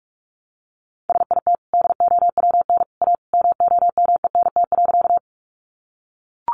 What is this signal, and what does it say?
Morse code, heard as a keyed tone, about 44 wpm, 700 Hz. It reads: SIT DOWN A MOMENT.
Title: Sentences from Top 1000 Words 45wpm